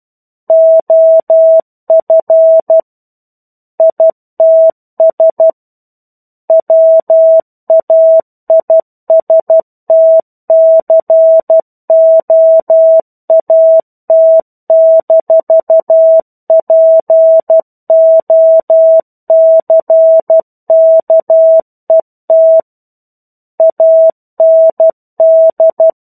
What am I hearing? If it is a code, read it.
Morse code, heard as a keyed tone, about 12 wpm, 650 Hz. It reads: OF ITS WAISTCOAT-POCKET AND
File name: 01-Chapter-01 12 WPM